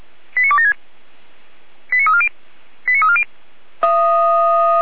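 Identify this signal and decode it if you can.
Five-tone selective call (ZVEI), heard as quick runs of five tones, tones of about 70 ms. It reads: 89276 89340 89340
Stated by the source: ZVEI five-tone selective call received by SDR, logged by feuerwehr-orlamuende.org fire brigade server.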